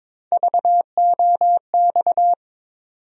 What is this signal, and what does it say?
Morse code, heard as a keyed tone, about 22 wpm, 700 Hz. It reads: VOX
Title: USA Call Sign Postfixes - 3 Letters 22wpm